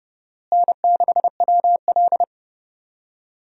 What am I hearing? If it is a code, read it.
Morse code, heard as a keyed tone, about 30 wpm, 700 Hz. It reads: N6WL
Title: USA Call Signs - 4 Characters - Repeated 3x - 30wpm